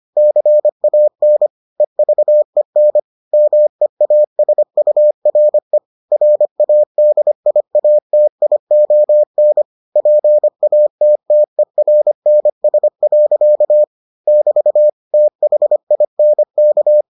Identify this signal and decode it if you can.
Morse code, heard as a keyed tone, about 25 wpm, 600 Hz. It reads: CAN EVEN MEASURE RADIATION PATTERNS. <BT> THINK